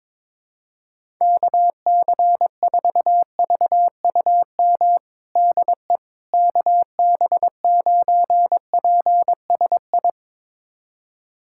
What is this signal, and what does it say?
Morse code, heard as a keyed tone, about 22 wpm, 700 Hz. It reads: KC4VUM DE KB9PSI
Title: USA Callsign DE Callsign - 6 Characters - 22wpm